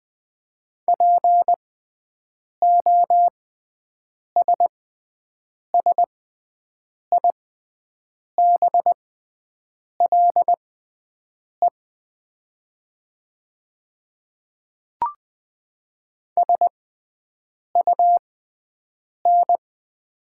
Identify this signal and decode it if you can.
Morse code, heard as a keyed tone, about 20 wpm, spaced as for 7 wpm, 700 Hz. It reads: POSSIBLE SUN
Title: 6x Character Spacing - Top 500 Words - 20wpm